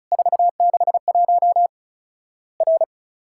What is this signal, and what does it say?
Morse code, heard as a keyed tone, about 34 wpm, 700 Hz. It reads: 461 R
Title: Worked All Europe DX Contest - QTC Exchange - 35wpm